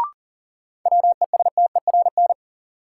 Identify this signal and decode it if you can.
Morse code, heard as a keyed tone, about 40 wpm, 700 Hz. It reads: WESTERN
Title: Sentences from Top 1000 Words - Review Part 7 - New Word Drill - 40wpm